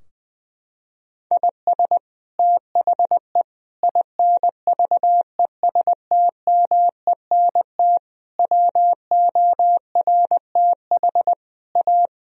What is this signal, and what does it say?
Morse code, heard as a keyed tone, about 20 wpm, 700 Hz. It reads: IS THE INVESTMENT WORTH A
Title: Sentences from Top 1000 Words - Review Part 4 - 20wpm